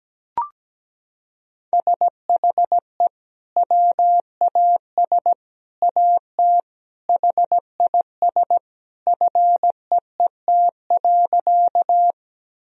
Morse code, heard as a keyed tone, about 17 wpm, 700 Hz. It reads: SHE WAS AT HIS FEET.
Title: Sentences from Top 1000 Words - Review Part 1 - 17wpm